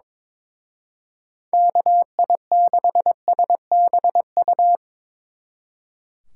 Morse code, heard as a keyed tone, about 25 wpm, 700 Hz. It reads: KI6SBU
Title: Speed Racing - USA 2x3 Call Signs 22wpm